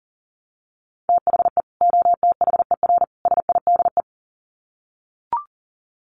Morse code, heard as a keyed tone, about 40 wpm, 700 Hz. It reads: THE OTHER SIDE
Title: Sets of 3 Words 40wpm